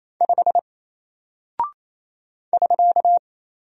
Morse code, heard as a keyed tone, about 27 wpm, 700 Hz. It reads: 5 <SK>
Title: Single Character (Letter:Number:Punctuation:Prosign) Alt 28wpm